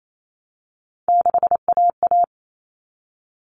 Morse code, heard as a keyed tone, about 28 wpm, 700 Hz. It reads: 6AA
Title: Common Cut Numbers - 3-Digit 28wpm